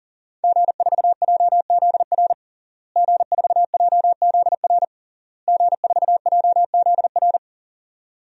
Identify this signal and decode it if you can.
Morse code, heard as a keyed tone, about 40 wpm, 700 Hz. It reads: G4JZR G4JZR G4JZR
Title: DXCC-100 Call Signs - 5 Characters - Repeated 3x - 40wpm